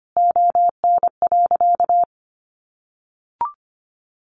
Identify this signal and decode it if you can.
Morse code, heard as a keyed tone, about 25 wpm, 700 Hz. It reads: ON.
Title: Easy Phrases 25wpm